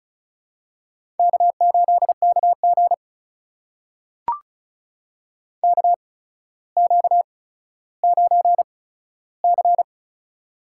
Morse code, heard as a keyed tone, about 36 wpm, spaced as for 15 wpm, 700 Hz. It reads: K8KG KQ9C
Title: USA Call Signs - 4 Characters - 8x Character Spacing - 35wpm